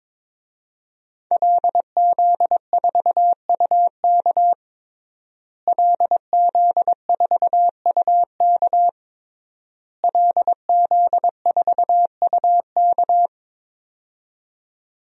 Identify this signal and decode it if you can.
Morse code, heard as a keyed tone, about 22 wpm, 700 Hz. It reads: LZ4UK LZ4UK LZ4UK
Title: DXCC-100 Call Signs - 5 Characters - Repeated 3x - 22wpm